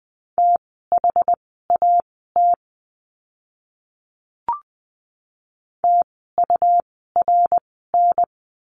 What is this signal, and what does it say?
Morse code, heard as a keyed tone, about 20 wpm, spaced as for 15 wpm, 700 Hz. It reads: THAT TURN
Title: 230 Lesson 37f - Letter x - Words 20wpm